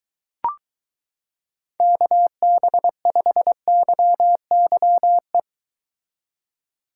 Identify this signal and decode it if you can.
Morse code, heard as a keyed tone, about 22 wpm, 700 Hz. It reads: KB5YYE